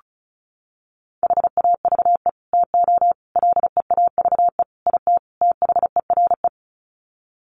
Morse code, heard as a keyed tone, about 34 wpm, 700 Hz. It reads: HAVE TO LEAVE IT THERE
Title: Sets of 5 Words Encore2 35wpm